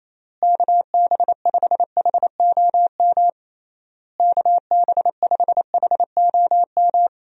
Morse code, heard as a keyed tone, about 27 wpm, 700 Hz. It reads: KB5HOM KB5HOM
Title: USA Call Signs - 6 Characters - Repeated 3x - 28wpm